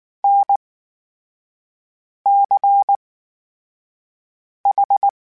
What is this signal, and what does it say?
Morse code, heard as a keyed tone, about 19 wpm, spaced as for 5 wpm, 800 Hz. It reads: NCH
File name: Special INGs / Special INGs-mp3s / VerySlowToFaster Ing @18